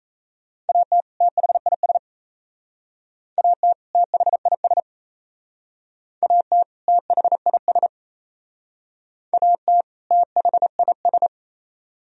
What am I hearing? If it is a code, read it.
Morse code, heard as a keyed tone, about 35 wpm, 700 Hz. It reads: AT THIS AT THIS AT THIS AT THIS